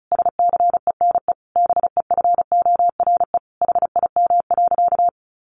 Morse code, heard as a keyed tone, about 34 wpm, 700 Hz. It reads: SCENE BEFORE HIM.
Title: Sentences from Top 1100 Words 35wpm